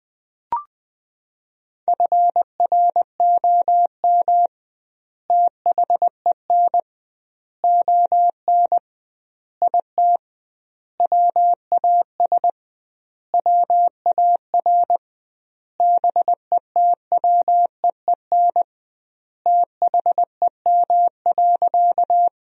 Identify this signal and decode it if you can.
Morse code, heard as a keyed tone, about 20 wpm, 700 Hz. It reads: FROM THEN ON IT WAS WAR BETWEEN THEM.
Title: 2x Word-Spacing - Sentences from Top 400 Words - 20wpm